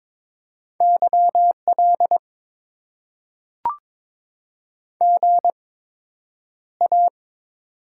Morse code, heard as a keyed tone, about 21 wpm, spaced as for 9 wpm, 700 Hz. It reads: YL GA
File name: Top 90 QSO Elements - 8x Character Spacing - 22wpm